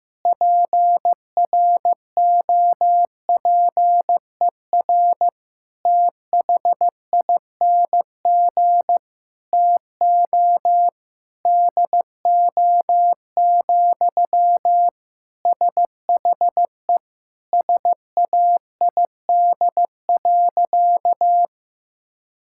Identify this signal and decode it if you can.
Morse code, heard as a keyed tone, about 15 wpm, 700 Hz. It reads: PROPER THING TO DO, SHE SAID.
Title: Sentences from Top 1000 Words - Review Part 5 - 15wpm